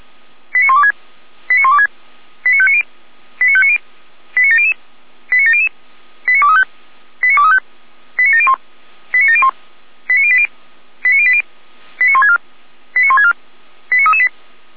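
Five-tone selective call (ZVEI), heard as quick runs of five tones, tones of about 70 ms. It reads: 89126 89126 89590 89590 89700 89700 89235 89235 89701 89701 89080 89080 89164 89164 89338
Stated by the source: ZVEI five-tone call received by SDR, from the server of Feuerwehr Orlamünde.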